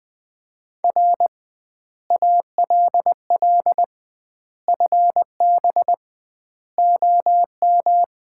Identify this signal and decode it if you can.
Morse code, heard as a keyed tone, about 20 wpm, 700 Hz. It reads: R ALL FB OM
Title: Common QSO Phrases - 2x Word-Spacing - 20wpm